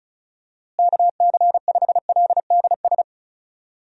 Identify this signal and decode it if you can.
Morse code, heard as a keyed tone, about 36 wpm, 700 Hz. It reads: KC5LDS
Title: USA Call Signs - 6 Characters - Repeated 3x - 35wpm